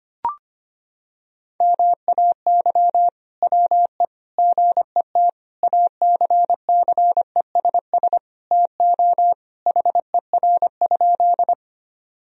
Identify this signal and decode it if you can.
Morse code, heard as a keyed tone, about 25 wpm, 700 Hz. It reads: MAY WE GET ACCESS TO HER?
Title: Sentences from Top 1000 Words - Review Part 4 - 25wpm